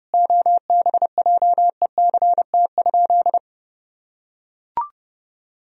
Morse code, heard as a keyed tone, about 30 wpm, 700 Hz. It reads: OBJECT?
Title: Sentences from Top 1000 Words - Review Part 3 - 30wpm